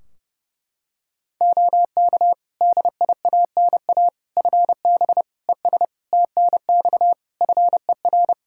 Morse code, heard as a keyed tone, about 30 wpm, 700 Hz. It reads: OK DIANA FB ES TNX FER
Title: Practice QSO - Exchange 2 - Info - 30wpm